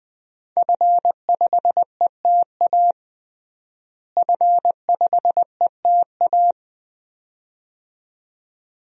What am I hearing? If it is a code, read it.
Morse code, heard as a keyed tone, about 21 wpm, 700 Hz. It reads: F5ETA F5ETA
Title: DXCC-100 Call Signs - 5 Characters - Repeated 2x - 20wpm